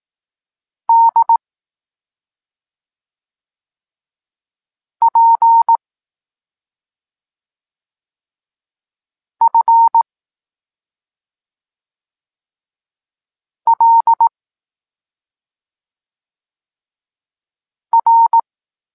Morse code, heard as a keyed tone, about 18 wpm, 920 Hz. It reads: D P F L R
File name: Tekst-26 1